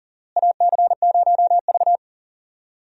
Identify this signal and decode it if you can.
Morse code, heard as a keyed tone, about 37 wpm, 700 Hz. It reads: AC0V
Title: USA Call Signs - 4 Characters - Repeated 3x - 40wpm